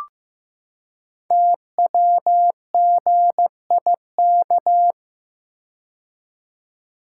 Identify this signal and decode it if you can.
Morse code, heard as a keyed tone, about 15 wpm, 700 Hz. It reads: TWGIK
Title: Code Groups - 5 Letters - 15wpm